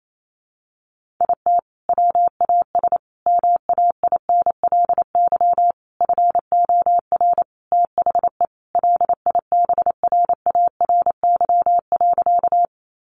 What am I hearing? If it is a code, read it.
Morse code, heard as a keyed tone, about 28 wpm, 700 Hz. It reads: IT WAS MAINLY FOR THE LIBRARY.
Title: Sentences from Top 1400 Words 28wpm